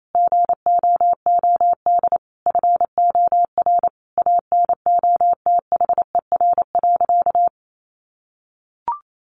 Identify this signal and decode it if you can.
Morse code, heard as a keyed tone, about 28 wpm, 700 Hz. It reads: GOOD FOR ANOTHER.
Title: Sentences from Top 1400 Words 28wpm